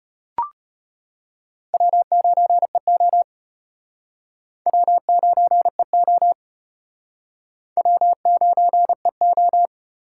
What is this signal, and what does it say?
Morse code, heard as a keyed tone, about 34 wpm, 700 Hz. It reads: W9EO W9EO W9EO